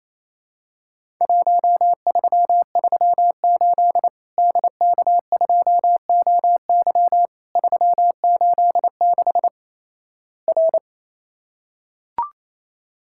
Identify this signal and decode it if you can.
Morse code, heard as a keyed tone, about 28 wpm, 700 Hz. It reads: 1338 DK2OY 386 R
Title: Worked All Europe DX Contest - QTC Exchange - 28wpm